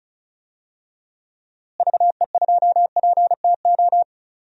Morse code, heard as a keyed tone, about 36 wpm, 700 Hz. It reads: VE2PTO
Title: DXCC-100 Call Signs - 6 Characters - Repeated 3x - 35wpm